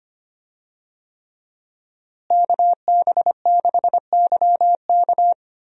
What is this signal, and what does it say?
Morse code, heard as a keyed tone, about 25 wpm, 700 Hz. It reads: KB6YK
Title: USA Call Signs - 5 Characters - Repeated 3x - 25wpm